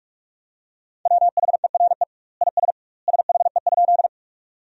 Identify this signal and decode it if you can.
Morse code, heard as a keyed tone, about 41 wpm, 700 Hz. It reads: WHERE IS SHE?
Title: 2x Word-Spacing - Sentences from Top 200 Words - 45wpm